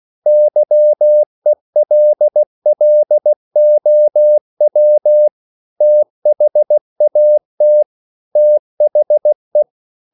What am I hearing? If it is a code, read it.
Morse code, heard as a keyed tone, about 16 wpm, 600 Hz. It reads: YELLOW THAT THE